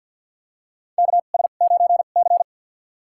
Morse code, heard as a keyed tone, about 51 wpm, spaced as for 36 wpm, 700 Hz. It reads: KS9C
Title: USA Call Signs - 4 Characters - 2x Character Spacing - 50wpm